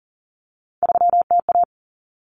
Morse code, heard as a keyed tone, about 40 wpm, 700 Hz. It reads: 3TA
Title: Common Cut Numbers - 3-Digit 40wpm